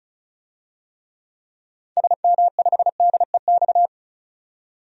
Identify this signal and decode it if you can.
Morse code, heard as a keyed tone, about 36 wpm, 700 Hz. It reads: SM5DEX
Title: DXCC-100 Call Signs - 6 Characters - 35wpm